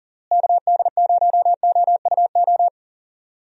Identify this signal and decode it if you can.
Morse code, heard as a keyed tone, about 40 wpm, 700 Hz. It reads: KD0OUO
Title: USA Call Signs - 6 Characters - Repeated 3x - 40wpm